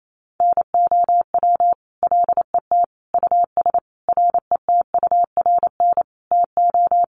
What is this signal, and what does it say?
Morse code, heard as a keyed tone, about 28 wpm, 700 Hz. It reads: NOW LET US RETURN TO
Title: Sentences from Top 800 Words 28wpm